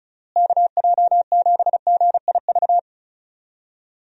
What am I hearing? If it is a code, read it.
Morse code, heard as a keyed tone, about 36 wpm, 700 Hz. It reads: KJ7GIV
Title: USA Call Signs - 6 Characters - Repeated 2x - 35wpm